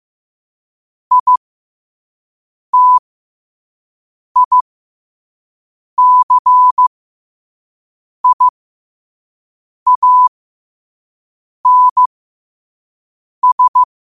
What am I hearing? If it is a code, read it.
Morse code, heard as a keyed tone, about 15 wpm, spaced as for 5 wpm, 1000 Hz. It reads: ITICIANS
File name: qotdfiles / 5wpm